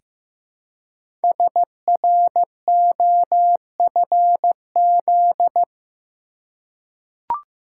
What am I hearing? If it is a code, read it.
Morse code, heard as a keyed tone, about 15 wpm, 700 Hz. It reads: SROFZ
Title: Code Groups - 5 Letters - 15wpm